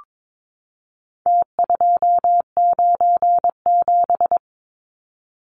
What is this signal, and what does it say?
Morse code, heard as a keyed tone, about 22 wpm, 700 Hz. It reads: T297
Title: Numbers - Contest Style - 4-Digit 22wpm